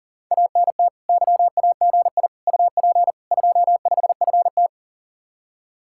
Morse code, heard as a keyed tone, about 52 wpm, 700 Hz. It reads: ANT YAGI UP 25FT